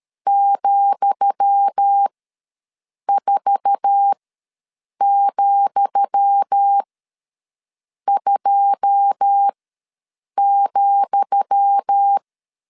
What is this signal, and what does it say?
Morse code, heard as a keyed tone, about 13 wpm, spaced as for 6 wpm, 800 Hz. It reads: ,4,2,
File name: LEZIONI / 10 SIGNS / 10D COMA ws 12wpm 3618